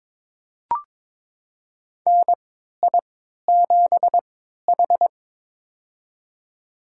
Morse code, heard as a keyed tone, about 21 wpm, spaced as for 12 wpm, 700 Hz. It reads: NI7H
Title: USA Call Signs - 3x Character Spacing - 22wpm